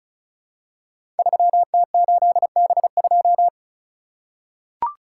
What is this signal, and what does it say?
Morse code, heard as a keyed tone, about 36 wpm, 690 Hz. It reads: 3T8B2
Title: Code Groups - 5 Letter:Numbers - 35wpm